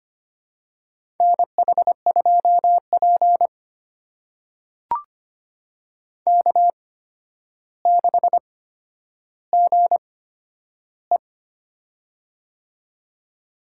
Morse code, heard as a keyed tone, about 25 wpm, spaced as for 11 wpm, 700 Hz. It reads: NH2P K6GE
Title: USA Call Signs - 4 Characters - 8x Character Spacing - 25wpm